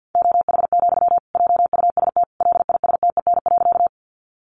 Morse code, heard as a keyed tone, about 50 wpm, 700 Hz. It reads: OH, JUST LISTEN.
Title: Sentences from Top 500 Words 50wpm